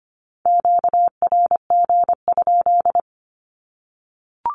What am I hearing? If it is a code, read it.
Morse code, heard as a keyed tone, about 25 wpm, 700 Hz. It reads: QRG?
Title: Q-Codes 25wpm